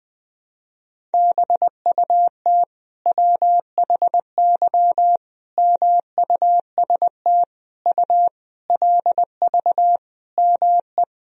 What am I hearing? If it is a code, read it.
Morse code, heard as a keyed tone, about 20 wpm, 700 Hz. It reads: BUT WHY MUST U LV ME
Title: Sentences from Top 400 Words and Abbreviations 20wpm